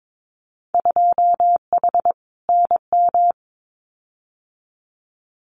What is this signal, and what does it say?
Morse code, heard as a keyed tone, about 22 wpm, 700 Hz. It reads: 2H NM
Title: Winter Field Day Exchanges Encore 22wpm